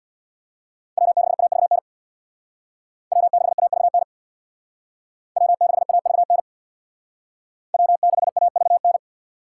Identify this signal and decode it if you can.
Morse code, heard as a keyed tone, about 62 wpm, 700 Hz. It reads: W6AVN W6AVN W6AVN W6AVN
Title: Speed Racing - USA 1x3 Call Signs 50wpm